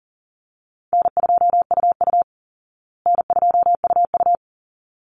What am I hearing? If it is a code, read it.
Morse code, heard as a keyed tone, about 40 wpm, 700 Hz. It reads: N2UU N2UU
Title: Call Signs Repeated 3x - 40wpm